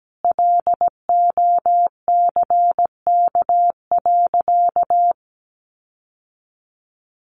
Morse code, heard as a keyed tone, about 17 wpm, 700 Hz. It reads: LOCK.
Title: Sentences from Top 1300 Words 17wpm